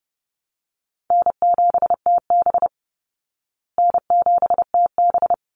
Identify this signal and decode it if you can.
Morse code, heard as a keyed tone, about 30 wpm, 700 Hz. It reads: N7TB N7TB
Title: Easy Call Signs Repeated 2x - 30wpm